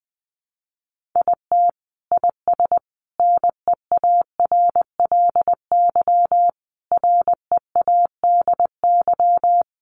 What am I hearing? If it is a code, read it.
Morse code, heard as a keyed tone, about 20 wpm, 700 Hz. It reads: IT IS NEARLY READY
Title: Sentences from Top 1500 Words 20wpm